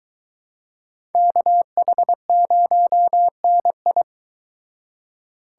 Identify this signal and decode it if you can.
Morse code, heard as a keyed tone, about 23 wpm, 700 Hz. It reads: KH0NI